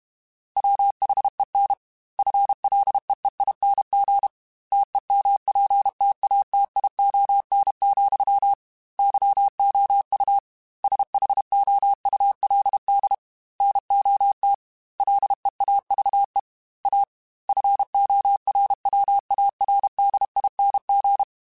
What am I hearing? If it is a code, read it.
Morse code, heard as a keyed tone, about 32 wpm, 800 Hz. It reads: WHEN FLEEING TEMPTATION, YOU SHOULD NOT LEAVE A FORWARDING
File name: Chuckles / Chuckles-mp3s / Temptation @30